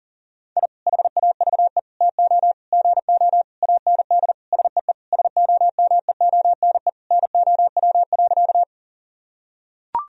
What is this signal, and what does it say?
Morse code, heard as a keyed tone, about 40 wpm, 700 Hz. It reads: I HAVE TO GO AND SEE SOMEONE NOW.
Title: Sentences from Top 1000 Words - Review Part 4 - 40wpm